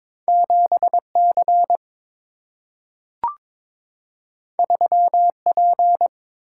Morse code, heard as a keyed tone, about 22 wpm, 700 Hz. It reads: 7C 3P
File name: practice / 22wpm / Two Letter-Number Combinations - 22wpm